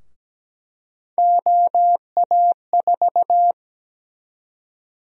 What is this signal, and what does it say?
Morse code, heard as a keyed tone, about 17 wpm, 700 Hz. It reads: OA4
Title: USA Call Sign Prefixes - 17wpm